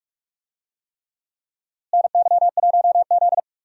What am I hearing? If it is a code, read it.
Morse code, heard as a keyed tone, about 46 wpm, 700 Hz. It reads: NY1Z
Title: USA Call Signs - 4 Characters - Repeated 3x - 45wpm